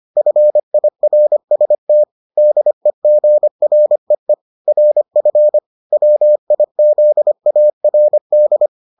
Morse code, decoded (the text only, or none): FIRST DEGREE RF WIZARD